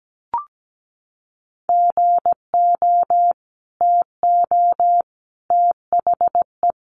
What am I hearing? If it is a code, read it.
Morse code, heard as a keyed tone, about 17 wpm, 700 Hz. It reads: GO TO THE
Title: Sentences from Top 1400 Words 17wpm